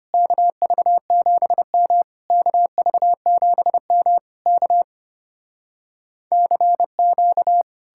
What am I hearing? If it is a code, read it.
Morse code, heard as a keyed tone, about 33 wpm, 700 Hz. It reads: KV7M KV7M K CQ